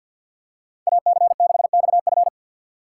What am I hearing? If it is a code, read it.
Morse code, heard as a keyed tone, about 52 wpm, 700 Hz. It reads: AC6XF
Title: USA Call Signs - 5 Characters - 8x Character Spacing - 50wpm